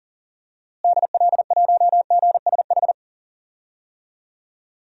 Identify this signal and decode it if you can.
Morse code, heard as a keyed tone, about 40 wpm, 700 Hz. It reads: DL1GSH
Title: DXCC-100 Call Signs - 6 Characters - Repeated 2x - 40wpm